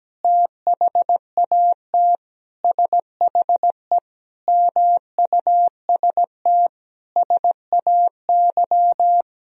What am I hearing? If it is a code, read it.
Morse code, heard as a keyed tone, about 17 wpm, 700 Hz. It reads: THAT SHE MUST SAY
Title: Sentences from Top 1000 Words - Review Part 3 - 17wpm